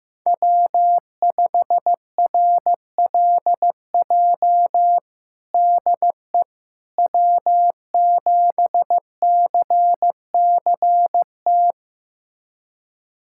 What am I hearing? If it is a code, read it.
Morse code, heard as a keyed tone, about 15 wpm, 700 Hz. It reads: W5RLJ DE W7CCT
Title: USA Callsign DE Callsign - 5 Characters - 15wpm